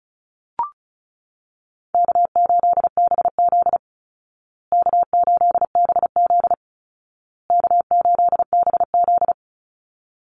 Morse code, heard as a keyed tone, about 34 wpm, 700 Hz. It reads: K8BZ K8BZ K8BZ
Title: Easy Call Signs Repeated 3x - 35wpm